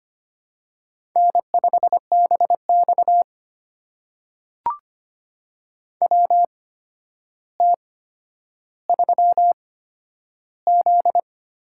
Morse code, heard as a keyed tone, about 25 wpm, spaced as for 11 wpm, 700 Hz. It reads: N5BX WT3Z
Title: USA Call Signs - 4 Characters - 8x Character Spacing - 25wpm